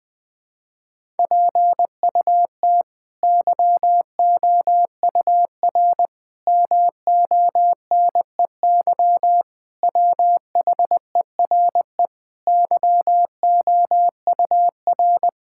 Morse code, heard as a keyed tone, about 20 wpm, 700 Hz. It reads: PUT YOUR MONEY WHERE YOUR
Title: Sets of 8 Words - 20wpm